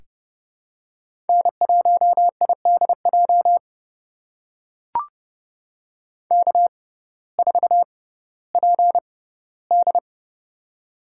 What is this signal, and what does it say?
Morse code, heard as a keyed tone, about 32 wpm, spaced as for 17 wpm, 700 Hz. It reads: N1IDJ K4PD
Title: USA Call Signs - 6x Character Spacing - 30wpm